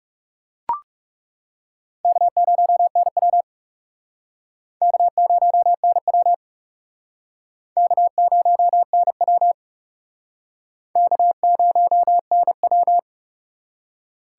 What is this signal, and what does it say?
Morse code, heard as a keyed tone, about 36 wpm, 700 Hz. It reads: K0NW K0NW K0NW K0NW
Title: Speed Racing - USA 1x2 Call Signs 30wpm